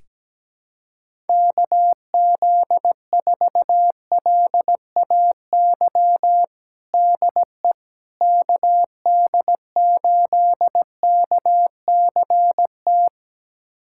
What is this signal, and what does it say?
Morse code, heard as a keyed tone, about 17 wpm, 700 Hz. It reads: KZ4LAY DE KD8KCT
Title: USA Callsign DE Callsign - 6 Characters - 17wpm